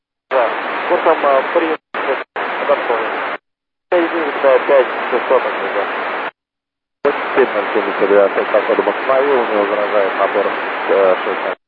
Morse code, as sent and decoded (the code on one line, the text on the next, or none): none